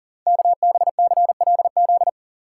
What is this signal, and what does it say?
Morse code, heard as a keyed tone, about 39 wpm, 690 Hz. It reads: KBCLZ
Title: Code Groups - 5 Letters - 40wpm